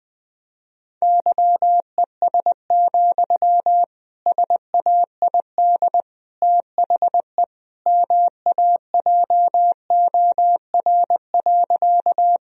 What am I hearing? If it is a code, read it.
Morse code, heard as a keyed tone, about 20 wpm, 700 Hz. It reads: YES, SAID THE MAJOR.
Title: Sentences from Top 1000 Words - Review Part 4 - 20wpm